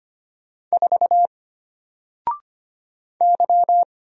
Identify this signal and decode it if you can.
Morse code, heard as a keyed tone, about 25 wpm, 700 Hz. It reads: 4 Y
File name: Single-Letter-Number-25wpm